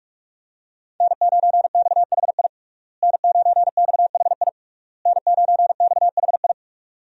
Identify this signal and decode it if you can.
Morse code, heard as a keyed tone, about 46 wpm, 700 Hz. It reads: N9XHI N9XHI N9XHI
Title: USA Call Signs - 5 Characters - Repeated 3x - 45wpm